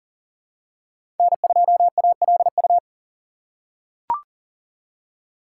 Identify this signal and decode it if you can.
Morse code, heard as a keyed tone, about 40 wpm, 700 Hz. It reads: N2ALU
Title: USA Call Signs - 5 Characters - 3x Character Spacing - 40wpm